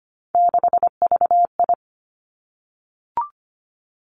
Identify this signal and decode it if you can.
Morse code, heard as a keyed tone, about 25 wpm, 700 Hz. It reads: 6VI
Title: Three Challenging Letter-Number Combinations 25wpm